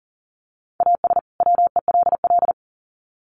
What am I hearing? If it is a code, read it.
Morse code, heard as a keyed tone, about 40 wpm, 700 Hz. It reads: AS WELL
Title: Sets of 2 Words 40wpm